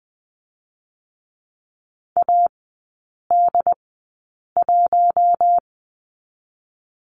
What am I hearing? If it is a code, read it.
Morse code, heard as a keyed tone, about 20 wpm, spaced as for 8 wpm, 700 Hz. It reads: AD1
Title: 087 Lesson 15e - Number 5 - Three Characters 20wpm